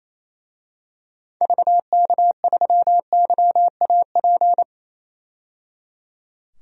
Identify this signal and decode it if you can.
Morse code, heard as a keyed tone, about 28 wpm, 700 Hz. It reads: VK3YAP
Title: DXCC-100 Call Signs - 6 Characters - 28wpm